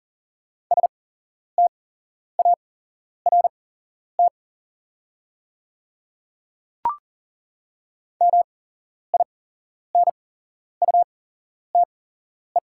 Morse code, heard as a keyed tone, about 40 wpm, spaced as for 11 wpm, 690 Hz. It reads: START MINUTE
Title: 8x Character Spacing - Top 500 Words - 40wpm